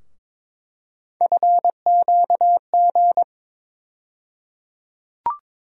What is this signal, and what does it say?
Morse code, heard as a keyed tone, about 21 wpm, 700 Hz. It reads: FQG
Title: USA Call Sign Postfixes - 3 Letters 22wpm